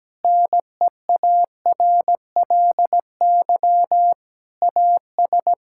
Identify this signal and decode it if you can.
Morse code, heard as a keyed tone, about 17 wpm, 700 Hz. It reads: NEARLY AS
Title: Sentences from Top 1600 Words 17wpm